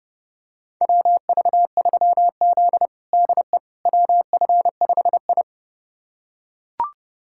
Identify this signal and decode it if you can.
Morse code, heard as a keyed tone, about 30 wpm, 700 Hz. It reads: WV3Z DE WF5I
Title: USA Callsign DE Callsign - 4 Characters - 30wpm